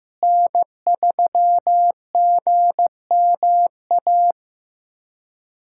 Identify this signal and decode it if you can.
Morse code, heard as a keyed tone, about 15 wpm, 700 Hz. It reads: N3GMA
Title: USA Callsign DE Callsign - 5 Characters - 15wpm